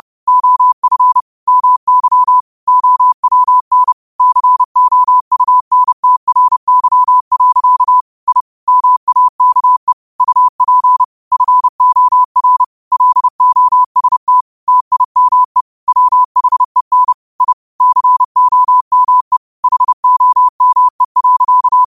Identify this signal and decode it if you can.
Morse code, heard as a keyed tone, about 30 wpm, 1000 Hz. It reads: OF MY OWN COUNTRY. I MAKE UP FOR LOST TIME WHEN I COME HOME.
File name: qotdfiles / 30wpm